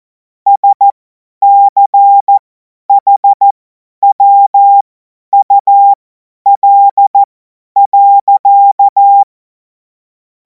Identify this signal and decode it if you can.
Morse code, heard as a keyed tone, about 14 wpm, spaced as for 10 wpm, 800 Hz. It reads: SCHWUL.
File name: Morsekurs / Witze / Witz 17. Februar 2026 50 F70